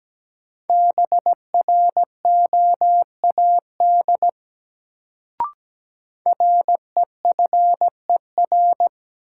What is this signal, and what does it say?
Morse code, heard as a keyed tone, about 17 wpm, 700 Hz. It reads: BROAD REFER
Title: Sentences from Top 1000 Words - New Word Drill - 17wpm